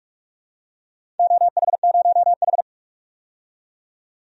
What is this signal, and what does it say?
Morse code, heard as a keyed tone, about 46 wpm, 700 Hz. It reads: OH0H
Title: DXCC-100 Call Signs - 4 Characters - 45wpm